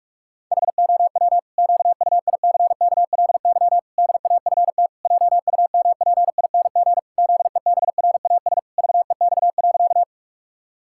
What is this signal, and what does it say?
Morse code, heard as a keyed tone, about 47 wpm, 700 Hz. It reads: HOW QUICKLY DAFT JUMPING ZEBRAS VEX.
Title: Pangram Challenge v2 - 45wpm